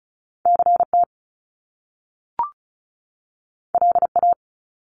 Morse code, heard as a keyed tone, about 36 wpm, 700 Hz. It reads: CT LA
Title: US States 35wpm